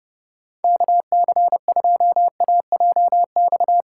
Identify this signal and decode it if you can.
Morse code, heard as a keyed tone, about 28 wpm, 700 Hz. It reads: KC2AJX